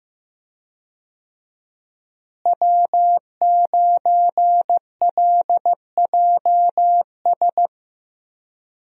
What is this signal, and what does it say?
Morse code, heard as a keyed tone, about 15 wpm, 700 Hz. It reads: W9LJS
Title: USA Call Signs - 5 Characters - Repeated 3x - 15wpm